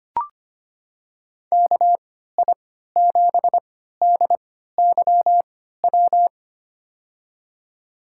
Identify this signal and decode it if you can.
Morse code, heard as a keyed tone, about 25 wpm, spaced as for 14 wpm, 700 Hz. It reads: KI7DYW
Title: USA Call Signs - 6 Characters - 3x Character Spacing - 25wpm